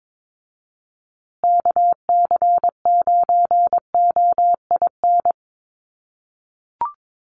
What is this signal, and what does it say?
Morse code, heard as a keyed tone, about 22 wpm, 700 Hz. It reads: KC9OIN
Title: USA Call Signs Repeated 2x 22wpm